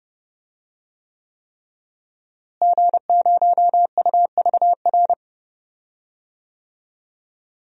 Morse code, heard as a keyed tone, about 32 wpm, 700 Hz. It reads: G0UVR
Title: DXCC-100 Call Signs - 5 Characters - Repeated 3x - 30wpm